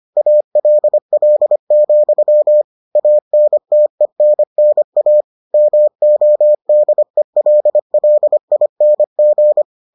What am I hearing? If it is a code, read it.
Morse code, heard as a keyed tone, about 25 wpm, 600 Hz. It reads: ALL, ANTENNA MODELLING